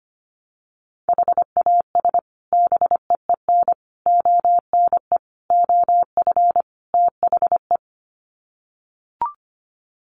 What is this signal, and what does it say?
Morse code, heard as a keyed tone, about 25 wpm, 700 Hz. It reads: HAS BEEN ONE OF THE
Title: Sets of 5 Words Encore 25wpm